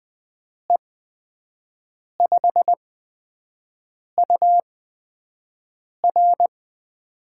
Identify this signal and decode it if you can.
Morse code, heard as a keyed tone, about 20 wpm, spaced as for 5 wpm, 700 Hz. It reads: E5UR
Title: USA Call Signs - 5 Characters - 8x Character Spacing - 20wpm